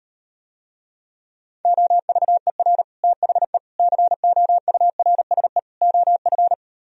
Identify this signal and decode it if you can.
Morse code, heard as a keyed tone, about 38 wpm, 700 Hz. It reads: OVER THE COURSE OF